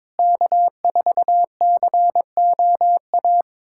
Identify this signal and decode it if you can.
Morse code, heard as a keyed tone, about 22 wpm, 700 Hz. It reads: K4COA
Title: USA Callsign DE Callsign - 5 Characters - 22wpm